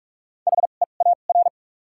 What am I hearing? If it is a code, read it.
Morse code, heard as a keyed tone, about 48 wpm, spaced as for 33 wpm, 700 Hz. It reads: HEAR
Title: 2x Character Spacing - Top 300 Words - 45wpm